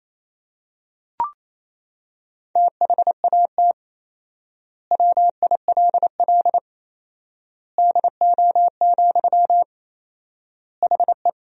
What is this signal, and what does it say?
Morse code, heard as a keyed tone, about 27 wpm, 700 Hz. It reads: THAT WILL DO, HE
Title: 4x Word-Spacing - Sentences from Top 100 Words - 28wpm